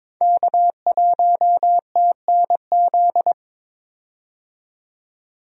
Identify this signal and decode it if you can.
Morse code, heard as a keyed tone, about 22 wpm, 700 Hz. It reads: K1TNZ
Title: USA Call Signs - 5 Characters - 22wpm